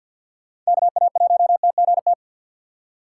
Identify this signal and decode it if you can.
Morse code, heard as a keyed tone, about 52 wpm, 700 Hz. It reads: KA1TGT
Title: USA Call Signs - 6 Characters - 8x Character Spacing - 50wpm